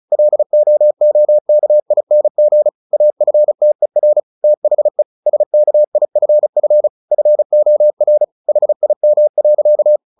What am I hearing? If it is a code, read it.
Morse code, heard as a keyed tone, about 34 wpm, 600 Hz. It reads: LOOKING AFTER THE SKIFF FOR HIM.
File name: the old man and the sea 2 35wpm